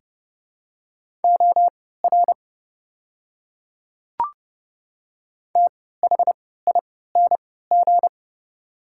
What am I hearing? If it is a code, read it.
Morse code, heard as a keyed tone, about 30 wpm, spaced as for 17 wpm, 700 Hz. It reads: OR THING
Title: 3x Character Spacing - Top 100 Words - 30wpm